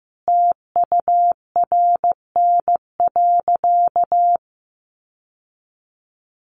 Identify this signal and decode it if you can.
Morse code, heard as a keyed tone, about 15 wpm, 700 Hz. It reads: TURN.
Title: Sentences from Top 1500 Words 15wpm